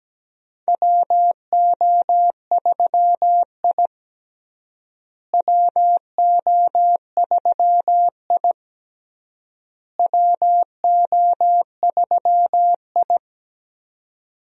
Morse code, heard as a keyed tone, about 17 wpm, 700 Hz. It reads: WO3I WO3I WO3I
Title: USA Call Signs - 4 Characters - Repeated 3x - 17wpm